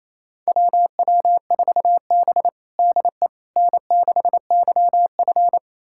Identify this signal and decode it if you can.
Morse code, heard as a keyed tone, about 28 wpm, 700 Hz. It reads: WW4B DE N6YF
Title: USA Callsign DE Callsign - 4 Characters - 28wpm